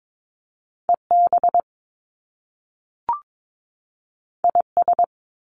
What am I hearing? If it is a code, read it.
Morse code, heard as a keyed tone, about 22 wpm, 700 Hz. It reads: EB IS
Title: Top Two Letter Combinations in Words 22wpm